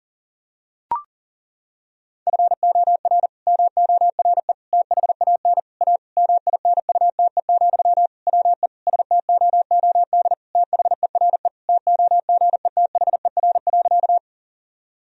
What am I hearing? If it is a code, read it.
Morse code, heard as a keyed tone, about 40 wpm, 700 Hz. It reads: FOR MORE THAN A MINUTE, WE STOOD THERE TOGETHER.
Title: Sentences from Top 1000 Words - Review Part 2 - 40wpm